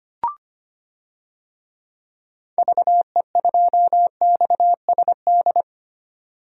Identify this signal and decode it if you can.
Morse code, heard as a keyed tone, about 25 wpm, 700 Hz. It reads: VE2XSD
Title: DXCC-100 Call Signs - 6 Characters - 25wpm